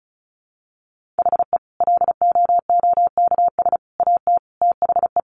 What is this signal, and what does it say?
Morse code, heard as a keyed tone, about 34 wpm, 700 Hz. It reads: HE LOOKS AT THE
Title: Additional Easy Phrases 35wpm